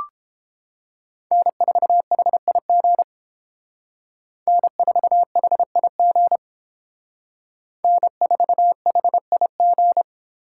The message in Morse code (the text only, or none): N4HIG N4HIG N4HIG